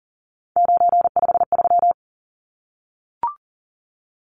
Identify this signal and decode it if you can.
Morse code, heard as a keyed tone, about 40 wpm, 700 Hz. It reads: 953
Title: Numbers - 3-Digit 40wpm